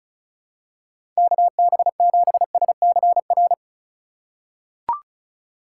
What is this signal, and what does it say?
Morse code, heard as a keyed tone, about 36 wpm, 700 Hz. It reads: KB7SCR
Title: USA Call Signs - 3x Character Spacing - 35wpm